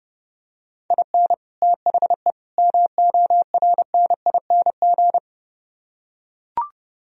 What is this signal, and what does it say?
Morse code, heard as a keyed tone, about 32 wpm, 700 Hz. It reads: IN THE MORNING